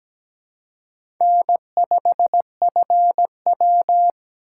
Morse code, heard as a keyed tone, about 20 wpm, 700 Hz. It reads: N5FW
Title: Speed Racing - USA 1x2 Call Signs 17wpm